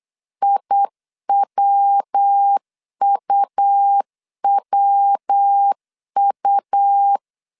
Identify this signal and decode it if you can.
Morse code, heard as a keyed tone, about 8 wpm, 800 Hz. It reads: IWUWU
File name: CW / DET / 1 3 1500C S B 8wpm 4141